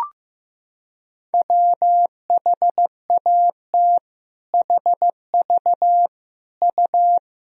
Morse code, heard as a keyed tone, about 15 wpm, 700 Hz. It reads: WHAT HV U
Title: Sentences from Top 400 Words and Abbreviations 15wpm